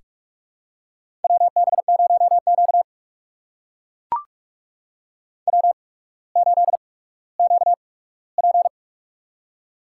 Morse code, heard as a keyed tone, about 44 wpm, spaced as for 19 wpm, 700 Hz. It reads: WB0Q W8QP
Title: USA Call Signs - 4 Characters - 8x Character Spacing - 45wpm